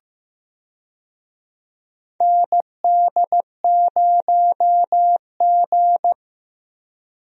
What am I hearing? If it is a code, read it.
Morse code, heard as a keyed tone, about 15 wpm, 700 Hz. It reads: ND0G
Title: USA Call Signs - 4 Characters - Repeated 2x - 15wpm